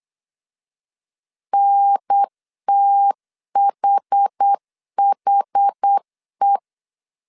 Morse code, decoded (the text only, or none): NTHHE